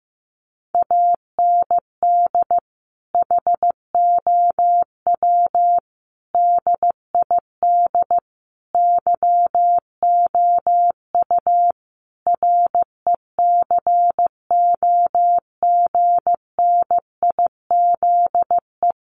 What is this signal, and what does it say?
Morse code, heard as a keyed tone, about 15 wpm, 700 Hz. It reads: AND HOW DID YOU RECOGNIZE
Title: Sentences from Top 1400 Words 15wpm